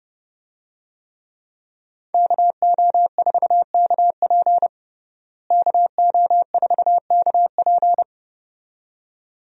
Morse code, heard as a keyed tone, about 32 wpm, 700 Hz. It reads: KO4KP KO4KP
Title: USA Call Signs - 5 Characters - Repeated 2x - 30wpm